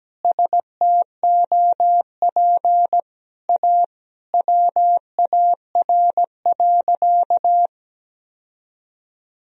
Morse code, heard as a keyed tone, about 17 wpm, 700 Hz. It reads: STOP A WAR.
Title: Sentences from Top 1000 Words - Review Part 2 - 17wpm